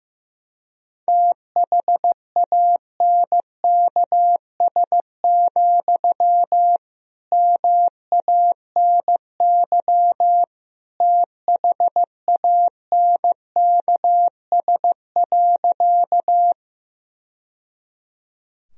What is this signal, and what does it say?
Morse code, heard as a keyed tone, about 15 wpm, 700 Hz. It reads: THANKS, MANY THANKS.
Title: Sentences from Top 1000 Words - Review Part 3 - 15wpm